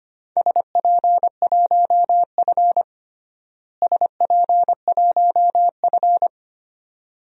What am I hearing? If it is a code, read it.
Morse code, heard as a keyed tone, about 25 wpm, 700 Hz. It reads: SP1F SP1F
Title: DXCC-100 Call Signs - 4 Characters - Repeated 2x - 25wpm